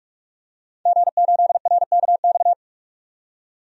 Morse code, heard as a keyed tone, about 44 wpm, 700 Hz. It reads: G8RKX
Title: DXCC-100 Call Signs - 5 Characters - 45wpm